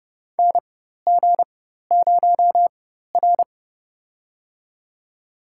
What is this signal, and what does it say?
Morse code, heard as a keyed tone, about 30 wpm, spaced as for 14 wpm, 700 Hz. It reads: NG0R
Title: USA Call Signs - 4 Characters - 4x Character Spacing - 30wpm